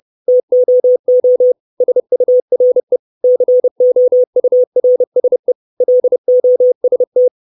Morse code, heard as a keyed tone, about 30 wpm, 500 Hz. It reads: TOO SURE COURSE LOST
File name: 632 Words 30WPM